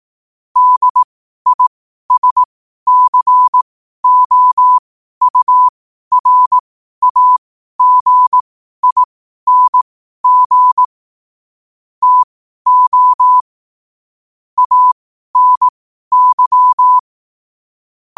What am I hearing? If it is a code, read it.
Morse code, heard as a keyed tone, about 18 wpm, spaced as for 12 wpm, 1000 Hz. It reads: DISCOURAGING TO ANY
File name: qotdfiles / 12wpm